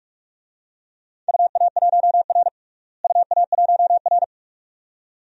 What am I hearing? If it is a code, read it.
Morse code, heard as a keyed tone, about 46 wpm, 700 Hz. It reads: UA1R UA1R
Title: DXCC-100 Call Signs - 4 Characters - Repeated 2x - 45wpm